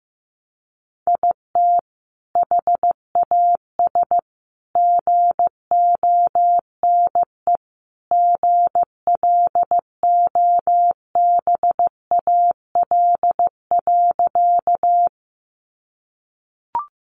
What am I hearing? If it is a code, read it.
Morse code, heard as a keyed tone, about 15 wpm, 700 Hz. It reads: IT HAS GONE GLOBAL.
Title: Sentences from Top 1200 Words 15wpm